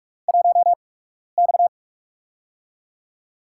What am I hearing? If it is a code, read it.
Morse code, heard as a keyed tone, about 44 wpm, spaced as for 12 wpm, 700 Hz. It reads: 1X
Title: USA Call Signs - 4 Characters - 8x Character Spacing - 45wpm